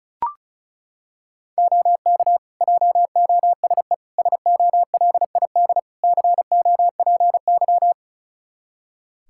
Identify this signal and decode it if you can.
Morse code, heard as a keyed tone, about 36 wpm, 700 Hz. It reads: OK JOSE SOLID COPY
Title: Practice QSO - Exchange 3 - More Info - 35wpm